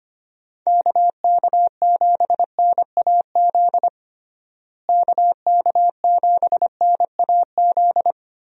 Morse code, heard as a keyed tone, about 25 wpm, 700 Hz. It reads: KK7NAZ KK7NAZ
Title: USA Call Signs - 6 Characters - Repeated 3x - 25wpm